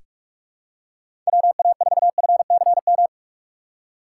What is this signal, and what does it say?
Morse code, heard as a keyed tone, about 42 wpm, 700 Hz. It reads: WA4FCM
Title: USA Call Signs - 2x Character Spacing - 45wpm